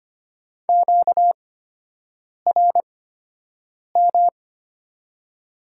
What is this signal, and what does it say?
Morse code, heard as a keyed tone, about 25 wpm, spaced as for 7 wpm, 690 Hz. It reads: QRM
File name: Top 90 QSO Elements - 8x Character Spacing - 25wpm